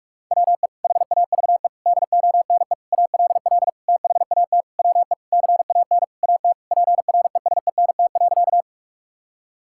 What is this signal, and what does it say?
Morse code, heard as a keyed tone, about 46 wpm, 700 Hz. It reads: WE HAVE DONE ALL THAT WE CAN AT PRESENT.
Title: Sentences from Top 1000 Words - Review Part 5 - 45wpm